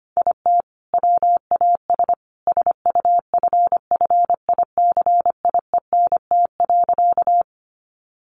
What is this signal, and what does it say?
Morse code, heard as a keyed tone, about 25 wpm, 700 Hz. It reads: IT WAS SUFFICIENT.
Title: Sentences from Top 1400 Words 25wpm